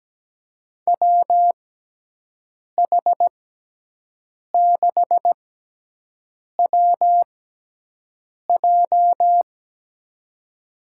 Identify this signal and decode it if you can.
Morse code, heard as a keyed tone, about 17 wpm, spaced as for 6 wpm, 700 Hz. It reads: WH6WJ
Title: USA Call Signs - 5 Characters - 6x Character Spacing - 17wpm